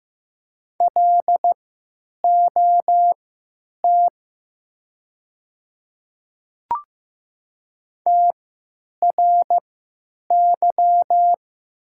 Morse code, heard as a keyed tone, about 15 wpm, spaced as for 9 wpm, 700 Hz. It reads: LOT TRY
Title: 3x Character Spacing - Top 500 Words - 15wpm